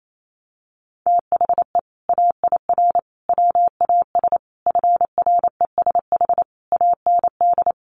Morse code, heard as a keyed tone, about 28 wpm, 700 Hz. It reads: THE AIR WAS FRESH AND
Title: Sentences from Top 1200 Words 28wpm